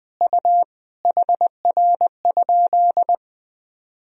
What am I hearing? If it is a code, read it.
Morse code, heard as a keyed tone, about 21 wpm, 700 Hz. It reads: U HR?
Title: Sentences from Top 200 Words and Abbreviations 20wpm